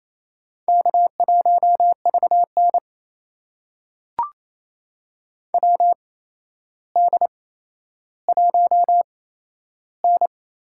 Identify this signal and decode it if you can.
Morse code, heard as a keyed tone, about 28 wpm, spaced as for 12 wpm, 700 Hz. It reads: K1VN WD1N
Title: USA Call Signs - 4 Characters - 8x Character Spacing - 28wpm